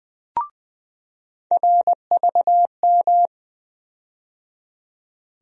Morse code, heard as a keyed tone, about 20 wpm, 700 Hz. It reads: RVM
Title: USA Call Sign Postfixes - 3 Letters 20wpm